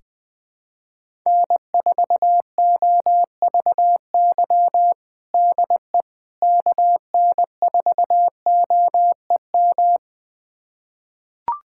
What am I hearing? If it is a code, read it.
Morse code, heard as a keyed tone, about 21 wpm, 700 Hz. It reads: N4OVY DE KN4OEM
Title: USA Callsign DE Callsign 20wpm